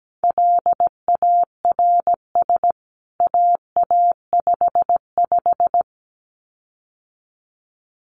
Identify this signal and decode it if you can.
Morse code, heard as a keyed tone, about 17 wpm, 700 Hz. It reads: LARS AA55
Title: CWT Exchange 17wpm